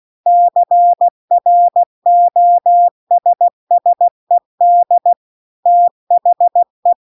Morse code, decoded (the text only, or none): CROSSED THE